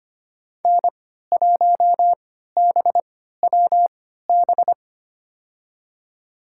Morse code, heard as a keyed tone, about 25 wpm, spaced as for 14 wpm, 700 Hz. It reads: N1BWB
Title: USA Call Signs - 5 Characters - 3x Character Spacing - 25wpm